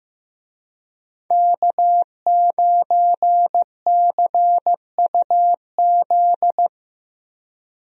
Morse code, heard as a keyed tone, about 15 wpm, 700 Hz. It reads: K9CUZ
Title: USA Call Signs - 5 Characters - Repeated 2x - 15wpm